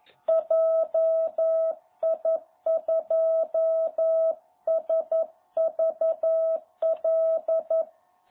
Morse code, heard as a keyed tone, about 11 wpm, 660 Hz. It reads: JI2SVL
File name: jh8cbh / 2021cw